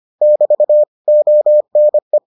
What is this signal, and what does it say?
Morse code, heard as a keyed tone, about 25 wpm, 600 Hz. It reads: <BT> ONE